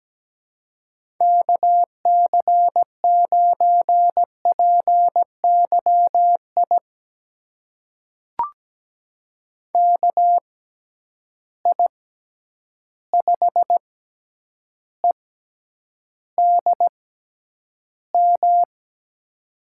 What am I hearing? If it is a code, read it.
Morse code, heard as a keyed tone, about 17 wpm, spaced as for 9 wpm, 700 Hz. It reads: KC9PYI KI5EDM
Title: USA Call Signs - 6 Characters - 6x Character Spacing - 17wpm